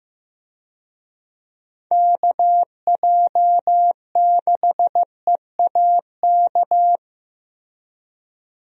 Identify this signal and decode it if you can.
Morse code, heard as a keyed tone, about 16 wpm, 700 Hz. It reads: KJ6EAK
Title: USA Call Signs - 6 Characters - Repeated 2x - 15wpm